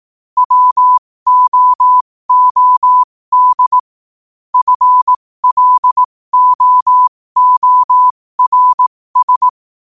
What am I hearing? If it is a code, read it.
Morse code, heard as a keyed tone, about 18 wpm, spaced as for 15 wpm, 1000 Hz. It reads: WOOD FLOORS
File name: qotdfiles / 15wpm